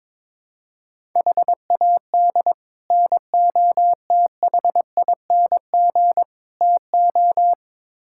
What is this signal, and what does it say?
Morse code, heard as a keyed tone, about 22 wpm, 700 Hz. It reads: HAD NOTHING TO
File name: practice / 22wpm / Build A Phrase - 5-Word Phrases - 22wpm